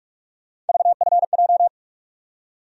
Morse code, heard as a keyed tone, about 46 wpm, 700 Hz. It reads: VFJ
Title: USA Call Sign Postfixes - 3 Letters 45wpm